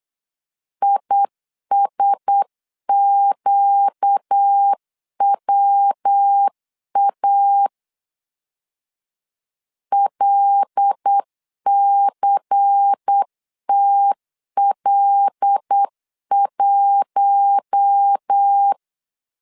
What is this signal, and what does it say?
Morse code, heard as a keyed tone, about 9 wpm, 800 Hz. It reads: ISQWA LCTL1
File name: CW / DET / 1 6 1500C S C 8wpm 5219